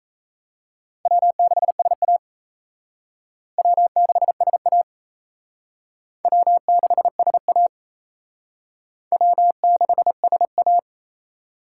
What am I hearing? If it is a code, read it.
Morse code, heard as a keyed tone, about 34 wpm, 700 Hz. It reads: W6SA W6SA W6SA W6SA
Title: Speed Racing - USA 1x2 Call Signs 28wpm